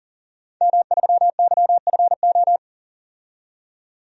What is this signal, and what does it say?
Morse code, heard as a keyed tone, about 39 wpm, 700 Hz. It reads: M3YFO
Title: DXCC-100 Call Signs - 5 Characters - 40wpm